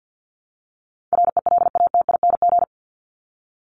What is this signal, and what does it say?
Morse code, heard as a keyed tone, about 50 wpm, 700 Hz. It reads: RELATING
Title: ING Suffix - Top 100 Words with Base Words 50wpm